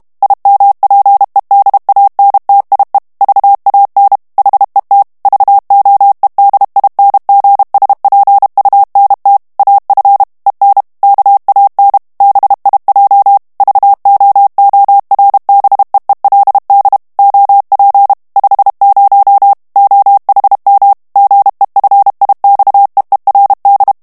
Morse code, heard as a keyed tone, about 32 wpm, 800 Hz. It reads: IMPEDANTIE VAN HET VOEDINGSPUNT AF EN KAN BIJ VOORBEELD OP 50 OHM GEFIXEERD
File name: morse mp3 / HOLLANDAIS texte clair 1 vitesse 30 mots minute